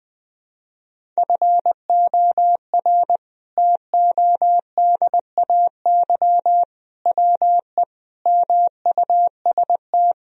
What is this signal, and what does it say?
Morse code, heard as a keyed tone, about 20 wpm, 700 Hz. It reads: FOR TODAY WE MUST
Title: Sentences from Top 1000 Words - Review Part 1 - 20wpm